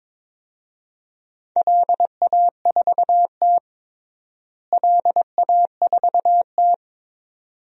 Morse code, heard as a keyed tone, about 22 wpm, 700 Hz. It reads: LA4T LA4T
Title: DXCC-100 Call Signs - 4 Characters - Repeated 2x - 22wpm